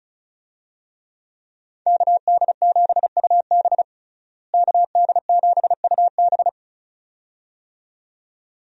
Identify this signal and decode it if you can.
Morse code, heard as a keyed tone, about 34 wpm, 700 Hz. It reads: KD7UB KD7UB
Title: USA Call Signs - 5 Characters - Repeated 2x - 35wpm